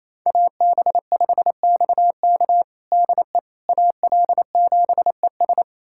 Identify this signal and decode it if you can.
Morse code, heard as a keyed tone, about 28 wpm, 700 Hz. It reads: AB5XK DE AL7ES
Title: USA Callsign DE Callsign - 5 Characters - 28wpm